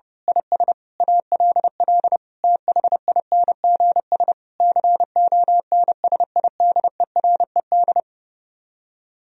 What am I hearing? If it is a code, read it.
Morse code, heard as a keyed tone, about 28 wpm, 700 Hz. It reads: IS ALL THINGS CONSIDERED
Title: Speed-Racing - Sets of 5 Words 30wpm